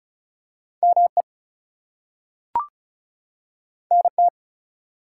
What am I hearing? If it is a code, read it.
Morse code, heard as a keyed tone, about 36 wpm, 700 Hz. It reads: ME NT
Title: US States and Canadian Provinces 35wpm